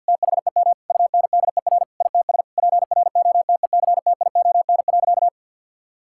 Morse code, heard as a keyed tone, about 52 wpm, 700 Hz. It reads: THEM UNDER ITS PROTECTION.
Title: Sentences from Top 1600 Words 50wpm